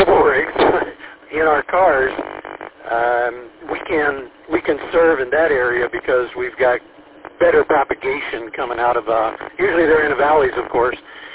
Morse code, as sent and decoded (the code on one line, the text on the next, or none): none